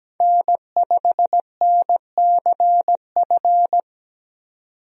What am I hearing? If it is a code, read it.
Morse code, heard as a keyed tone, about 17 wpm, 700 Hz. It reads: N5NCF
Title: USA Call Signs - 5 Characters - 17wpm